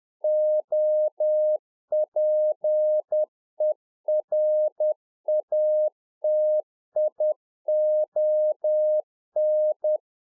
Morse code, heard as a keyed tone, about 10 wpm, 610 Hz. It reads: OPERATION